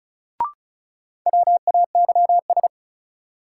Morse code, heard as a keyed tone, about 36 wpm, 700 Hz. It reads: WAYS
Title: Sets of 5 Words Encore - New Word Drill - 35wpm